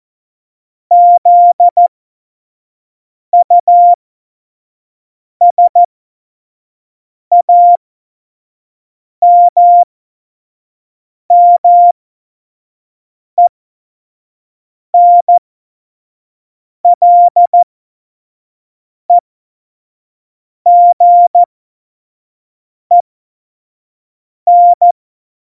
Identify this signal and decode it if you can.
Morse code, heard as a keyed tone, about 14 wpm, spaced as for 5 wpm, 700 Hz. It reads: ZUSAMMENLEGEN
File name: Morsekurs / Witze / Witz 28 Juli 2020, 25 F70